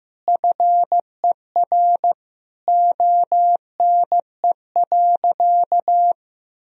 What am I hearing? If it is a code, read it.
Morse code, heard as a keyed tone, about 15 wpm, 700 Hz. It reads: FER ONE.
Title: Sentences from Top 300 Words and Abbreviations 15wpm